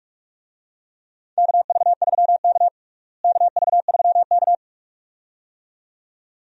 Morse code, heard as a keyed tone, about 46 wpm, 700 Hz. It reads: KV3K KV3K
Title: USA Call Signs - 4 Characters - Repeated 2x - 45wpm